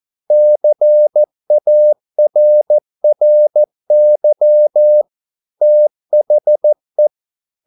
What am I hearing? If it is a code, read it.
Morse code, heard as a keyed tone, about 13 wpm, 600 Hz. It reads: CARRY THE